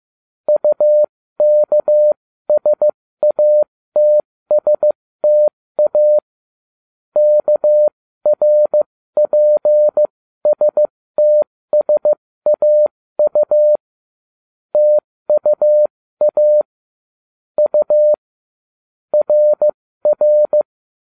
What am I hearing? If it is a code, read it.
Morse code, heard as a keyed tone, about 15 wpm, spaced as for 13 wpm, 600 Hz. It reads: UKSATSTA KRPSTSAU TUA U RR